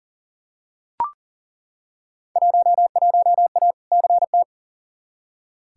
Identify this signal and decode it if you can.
Morse code, heard as a keyed tone, about 40 wpm, 700 Hz. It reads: 11A CT
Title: Field Day Exchanges Encore - 40wpm